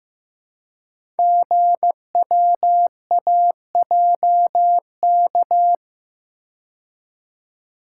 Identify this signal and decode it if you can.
Morse code, heard as a keyed tone, about 15 wpm, 700 Hz. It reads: GWAJK
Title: Code Groups - 5 Letter:Numbers - 15wpm